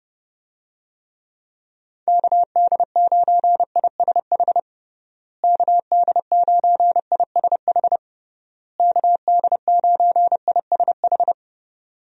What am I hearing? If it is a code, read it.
Morse code, heard as a keyed tone, about 30 wpm, 700 Hz. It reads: KD9ISH KD9ISH KD9ISH
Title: USA Call Signs - 6 Characters - Repeated 3x - 30wpm